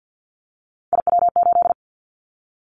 Morse code, heard as a keyed tone, about 50 wpm, 700 Hz. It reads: IW8
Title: Callsign Prefixes for Top 50 DXCC - 50wpm